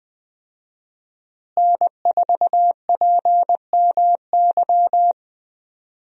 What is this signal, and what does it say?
Morse code, heard as a keyed tone, about 20 wpm, 700 Hz. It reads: N4PMY
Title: USA Call Signs - 5 Characters - Repeated 2x - 20wpm